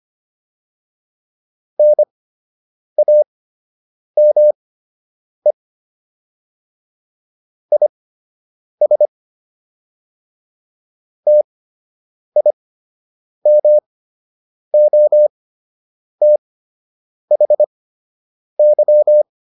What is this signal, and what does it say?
Morse code, heard as a keyed tone, about 25 wpm, spaced as for 8 wpm, 600 Hz. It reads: NAME IS TIMOTHY
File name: QSO 101 08